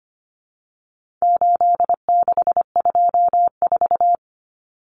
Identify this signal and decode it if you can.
Morse code, heard as a keyed tone, about 25 wpm, 700 Hz. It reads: 8624
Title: Common Cut Numbers - 4-Digit 25wpm